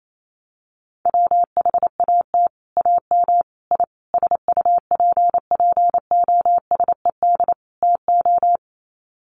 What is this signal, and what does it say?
Morse code, heard as a keyed tone, about 28 wpm, 700 Hz. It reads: WHAT AM I SUPPOSED TO
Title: Sets of 5 Words Encore2 28wpm